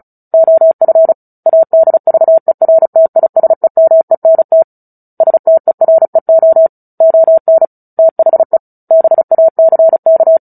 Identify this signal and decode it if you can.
Morse code, heard as a keyed tone, about 34 wpm, 650 Hz. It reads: OF ADVERTISEMENT STEREO ON THE BACK
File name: Book Two / 09-Chapter 9 - 35 WPM